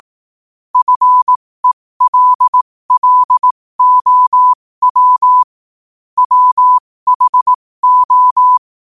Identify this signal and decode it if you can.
Morse code, heard as a keyed tone, about 18 wpm, spaced as for 15 wpm, 1000 Hz. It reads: FELLOW WHO
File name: qotdfiles / 15wpm